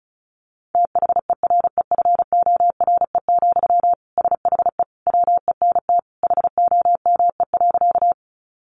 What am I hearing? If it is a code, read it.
Morse code, heard as a keyed tone, about 34 wpm, 700 Hz. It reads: THEREFORE, SHE WENT HOME.
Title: Sentences from Top 800 Words 35wpm